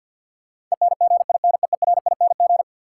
Morse code, heard as a keyed tone, about 53 wpm, 700 Hz. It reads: ENGINEERING
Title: Sentences from Top 1500 Words - New Word Drill - 50wpm